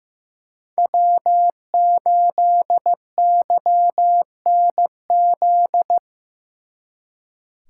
Morse code, heard as a keyed tone, about 18 wpm, 700 Hz. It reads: W8YNZ
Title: Speed Racing - USA Call Signs 15wpm